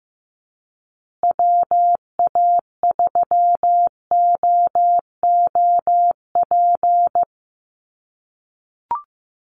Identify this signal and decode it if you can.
Morse code, heard as a keyed tone, about 15 wpm, 700 Hz. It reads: WA3OOP
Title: USA Call Signs Repeated 3x 15wpm